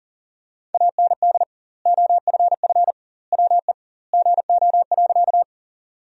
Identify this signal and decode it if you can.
Morse code, heard as a keyed tone, about 40 wpm, 700 Hz. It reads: AND OFF WE GO.
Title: 2x Word-Spacing - Sentences from Top 200 Words - 40wpm